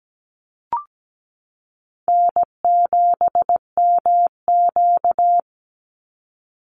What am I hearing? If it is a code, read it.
Morse code, heard as a keyed tone, about 17 wpm, 700 Hz. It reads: N7MQ
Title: Easy Call Signs - 17wpm